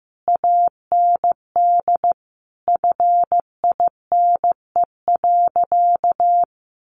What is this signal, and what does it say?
Morse code, heard as a keyed tone, about 14 wpm, 700 Hz. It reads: AND FINE.
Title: Sentences from Top 1300 Words 15wpm